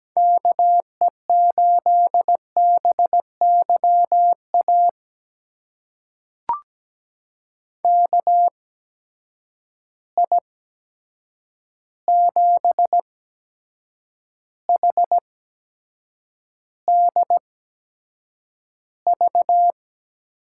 Morse code, heard as a keyed tone, about 17 wpm, spaced as for 7 wpm, 700 Hz. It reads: KE8BYA KI7HDV
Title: USA Call Signs - 6 Characters - 8x Character Spacing - 17wpm